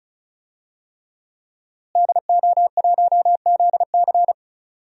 Morse code, heard as a keyed tone, about 34 wpm, 700 Hz. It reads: DO1ZC
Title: DXCC-100 Call Signs - 5 Characters - Repeated 3x - 35wpm